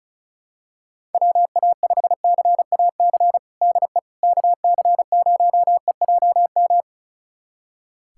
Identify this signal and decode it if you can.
Morse code, heard as a keyed tone, about 36 wpm, 700 Hz. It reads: WA5CAC DE KC0EJM
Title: USA Callsign DE Callsign 35wpm